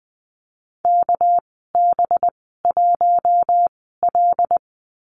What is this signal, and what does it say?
Morse code, heard as a keyed tone, about 20 wpm, spaced as for 15 wpm, 700 Hz. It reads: KB1L
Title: 213 Lesson 34g - Number 8 - Call Signs 20wpm